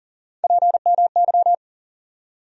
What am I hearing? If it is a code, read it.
Morse code, heard as a keyed tone, about 40 wpm, 700 Hz. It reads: PMY
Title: USA Call Sign Postfixes - 3 Letters 40wpm